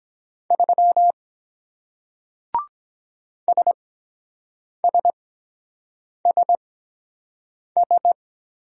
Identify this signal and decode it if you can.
Morse code, heard as a keyed tone, about 23 wpm, spaced as for 7 wpm, 700 Hz. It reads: 3 SSSS